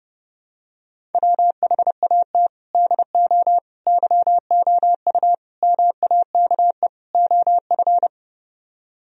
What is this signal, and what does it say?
Morse code, heard as a keyed tone, about 32 wpm, 700 Hz. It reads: WHAT DO YOU MAKE OF